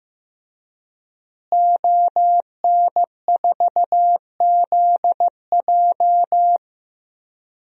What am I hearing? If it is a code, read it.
Morse code, heard as a keyed tone, about 15 wpm, 700 Hz. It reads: ON4ZJ
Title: DXCC-100 Call Signs - 5 Characters - Repeated 2x - 15wpm